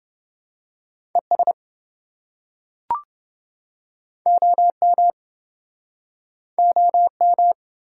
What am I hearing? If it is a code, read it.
Morse code, heard as a keyed tone, about 28 wpm, 700 Hz. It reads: ES OM OM